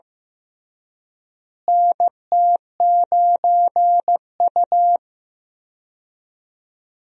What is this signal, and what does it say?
Morse code, heard as a keyed tone, about 15 wpm, 700 Hz. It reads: NT9U
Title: USA Call Signs - 4 Characters - Repeated 3x - 15wpm